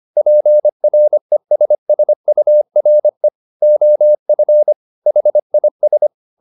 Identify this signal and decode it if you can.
Morse code, heard as a keyed tone, about 25 wpm, 600 Hz. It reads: PRESSURE OF HIS
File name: the old man and the sea part 1 25wpm